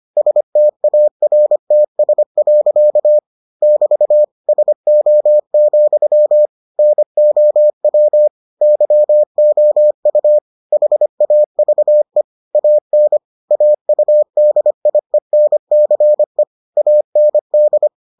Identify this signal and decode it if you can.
Morse code, heard as a keyed tone, about 25 wpm, 600 Hz. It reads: STARTS. <BT> SO, NOW YOU HAVE AN AUDIENCE AND